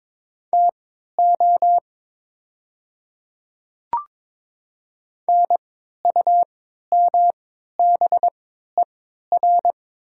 Morse code, heard as a keyed tone, about 22 wpm, spaced as for 12 wpm, 700 Hz. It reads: TO NUMBER
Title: 3x Character Spacing - Top 100 Words - 22wpm